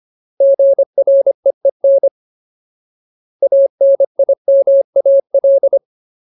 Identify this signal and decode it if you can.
Morse code, heard as a keyed tone, about 25 wpm, 550 Hz. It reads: GREEN ANIMAL
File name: Ultimate Word List / 25WPM Ultimate